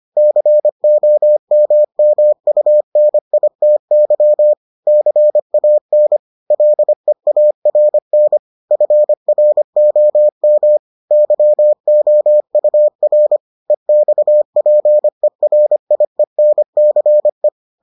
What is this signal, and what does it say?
Morse code, heard as a keyed tone, about 25 wpm, 600 Hz. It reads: COMMUNITY CAN LEARN FROM YOUR EXPERIENCE